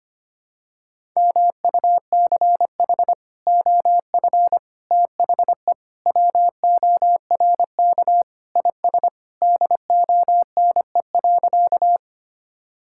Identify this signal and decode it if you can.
Morse code, heard as a keyed tone, about 25 wpm, 700 Hz. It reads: MUCH OF THE WORK IS DONE.
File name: practice / 25wpm / Sentences with 3-Word Phrases - 25wpm